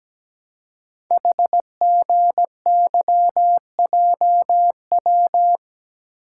Morse code, heard as a keyed tone, about 17 wpm, 700 Hz. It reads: HGYJW
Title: Code Groups - 5 Letters - 17wpm